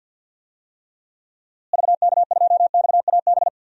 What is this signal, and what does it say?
Morse code, heard as a keyed tone, about 50 wpm, 700 Hz. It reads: VK2XAB
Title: DXCC-100 Call Signs - 6 Characters - Repeated 2x - 50wpm